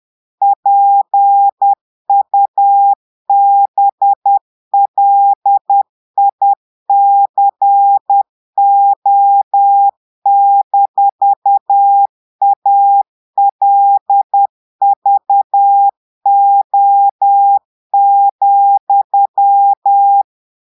Morse code, heard as a keyed tone, about 10 wpm, 800 Hz. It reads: PUBLICO-ALVO,
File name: portugais texte en clair 2 vitesse 10 mots minutes